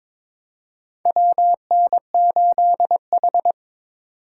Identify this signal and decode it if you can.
Morse code, heard as a keyed tone, about 22 wpm, 700 Hz. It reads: WN8H
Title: USA Call Signs - 4 Characters - 3x Character Spacing - 22wpm